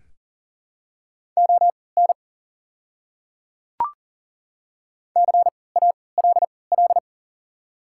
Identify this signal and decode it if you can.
Morse code, heard as a keyed tone, about 40 wpm, spaced as for 23 wpm, 690 Hz. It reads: ON CALL
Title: 3x Character Spacing - Top 100 Words - 40wpm